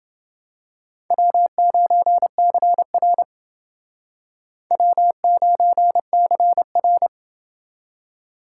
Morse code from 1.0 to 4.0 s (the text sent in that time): W9CR